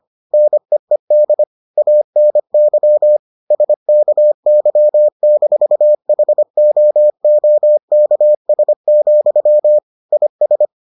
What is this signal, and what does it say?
Morse code, heard as a keyed tone, about 25 wpm, 600 Hz. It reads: NEED ANY SKY-HOOKS, IS